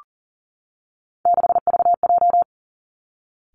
Morse code, heard as a keyed tone, about 37 wpm, 700 Hz. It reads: 6VJ
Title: Three Challenging Letter-Number Combinations 40wpm